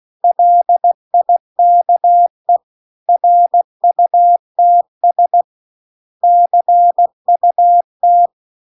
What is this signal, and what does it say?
Morse code, heard as a keyed tone, about 16 wpm, 700 Hz. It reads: LIKE RUTS CUT